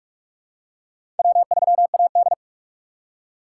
Morse code, heard as a keyed tone, about 46 wpm, 700 Hz. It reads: W3AD
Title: USA Call Signs - 2x Character Spacing - 45wpm